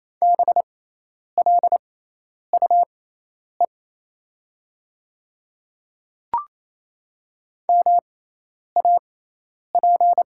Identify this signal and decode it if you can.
Morse code, heard as a keyed tone, about 28 wpm, spaced as for 10 wpm, 700 Hz. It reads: BLUE MAP
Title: 6x Character Spacing - Top 500 Words - 28wpm